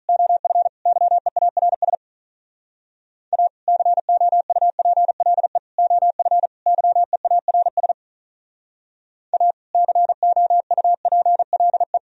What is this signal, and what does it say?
Morse code, heard as a keyed tone, about 46 wpm, 700 Hz. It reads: OF YEARS A COUPLE OF YEARS A COUPLE